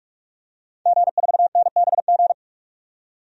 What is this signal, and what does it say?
Morse code, heard as a keyed tone, about 41 wpm, 700 Hz. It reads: G4NBG
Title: DXCC-100 Call Signs - 5 Characters - 45wpm